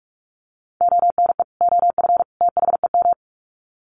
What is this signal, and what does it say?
Morse code, heard as a keyed tone, about 44 wpm, 700 Hz. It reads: ONE OF THEM
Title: Sets of 3 Words 45wpm